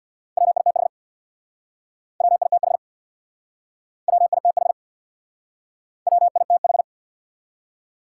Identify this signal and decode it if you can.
Morse code, heard as a keyed tone, about 62 wpm, 700 Hz. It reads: WITH WITH WITH WITH
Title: Speed-Racing Top 180 QSO Elements 50wpm